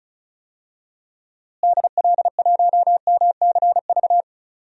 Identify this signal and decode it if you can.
Morse code, heard as a keyed tone, about 36 wpm, 700 Hz. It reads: DL1MCV
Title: DXCC-100 Call Signs - 6 Characters - Repeated 3x - 35wpm